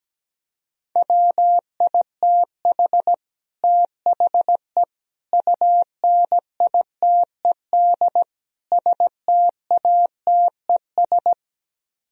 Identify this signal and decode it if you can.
Morse code, heard as a keyed tone, about 17 wpm, 700 Hz. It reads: WITH THE UNITED STATES
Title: Speed-Racing - Sets of 4 Words 17wpm